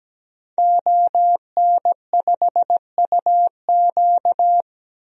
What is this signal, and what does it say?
Morse code, heard as a keyed tone, about 17 wpm, 700 Hz. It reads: ON5UQ
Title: DXCC-100 Call Signs - 5 Characters - 17wpm